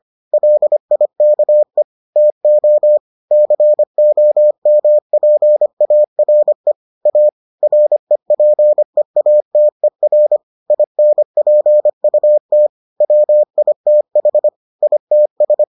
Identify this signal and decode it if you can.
Morse code, heard as a keyed tone, about 25 wpm, 600 Hz. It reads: LIKE TO COMPARE A REPEATER INPUT WITH ITS